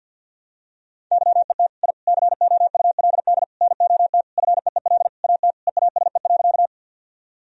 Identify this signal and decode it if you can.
Morse code, heard as a keyed tone, about 51 wpm, 700 Hz. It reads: YET I COULD NOT FEEL AT EASE.
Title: Sentences from Top 1000 Words - Review Part 1 - 50wpm